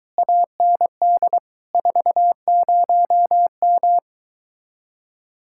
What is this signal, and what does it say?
Morse code, heard as a keyed tone, about 23 wpm, 700 Hz. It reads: AND 40M